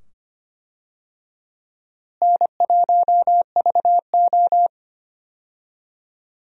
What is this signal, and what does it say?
Morse code, heard as a keyed tone, about 25 wpm, 700 Hz. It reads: N1VO
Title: USA Call Signs - 4 Characters - Repeated 2x - 25wpm